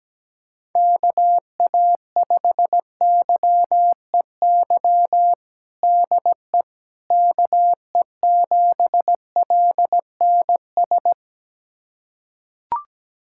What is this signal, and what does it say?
Morse code, heard as a keyed tone, about 17 wpm, 700 Hz. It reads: KA5YEY DE KE7LNS
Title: USA Callsign DE Callsign 17wpm